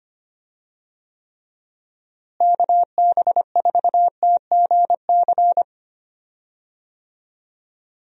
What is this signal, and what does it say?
Morse code, heard as a keyed tone, about 25 wpm, 700 Hz. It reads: KB4TGC
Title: USA Call Signs - 6 Characters - Repeated 2x - 25wpm